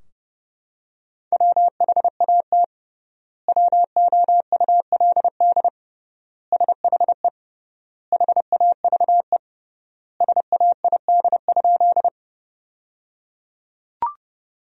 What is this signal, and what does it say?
Morse code, heard as a keyed tone, about 30 wpm, 700 Hz. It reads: WHAT WOULD SHE HAVE SAID?
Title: 3x Word-Spacing - Sentences from Top 100 Words - 30wpm